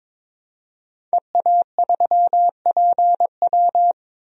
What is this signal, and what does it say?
Morse code, heard as a keyed tone, about 22 wpm, 700 Hz. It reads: EA3PW
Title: DXCC-100 Call Signs - 5 Characters - 22wpm